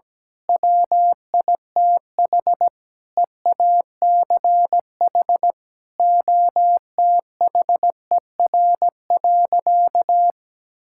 Morse code, heard as a keyed tone, about 17 wpm, 700 Hz. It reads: WITH EACH OTHER.
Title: Sentences from Top 1000 Words - Review Part 6 - 17wpm